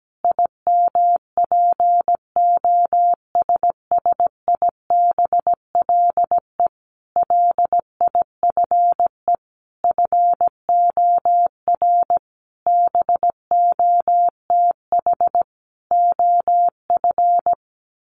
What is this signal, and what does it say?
Morse code, heard as a keyed tone, about 17 wpm, 700 Hz. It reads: IMPOSSIBLE LIFE FOR BOTH OF
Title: Sentences from Top 1100 Words 17wpm